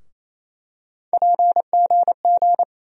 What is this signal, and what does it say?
Morse code, heard as a keyed tone, about 28 wpm, 700 Hz. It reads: PGG
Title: USA Call Sign Postfixes - 3 Letters 28wpm